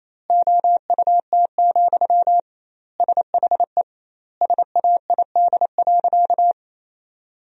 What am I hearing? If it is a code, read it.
Morse code, heard as a keyed tone, about 28 wpm, 700 Hz. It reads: OUT, SHE SAID.
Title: 2x Word-Spacing - Sentences from Top 500 Words - 28wpm